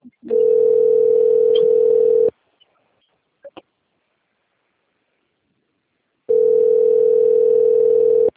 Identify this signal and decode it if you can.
Ringback tone (North America), 440 + 480 Hz.